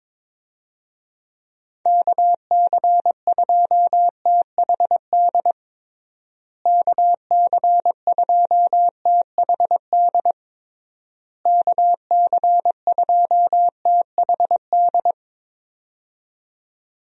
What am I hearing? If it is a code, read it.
Morse code, heard as a keyed tone, about 22 wpm, 700 Hz. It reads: KC2THD KC2THD KC2THD
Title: USA Call Signs - 6 Characters - Repeated 3x - 22wpm